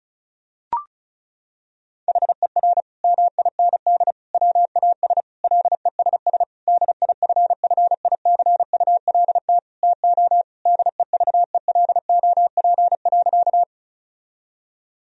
Morse code, heard as a keyed tone, about 36 wpm, 700 Hz. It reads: HER MIND WAS LESS DIFFICULT TO DEVELOP.
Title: Sentences from Top 1000 Words - Review Part 2 - 35wpm